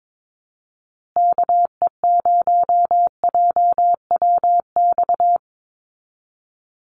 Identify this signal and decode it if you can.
Morse code, heard as a keyed tone, about 22 wpm, 700 Hz. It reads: KE0JWX
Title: USA Call Signs Repeated 2x 22wpm